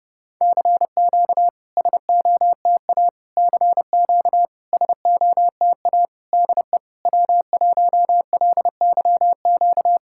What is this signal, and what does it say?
Morse code, heard as a keyed tone, about 30 wpm, 700 Hz. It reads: CQ SOTA CQ SOTA DE W1LYQ
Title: SOTA QSO Practice 30wpm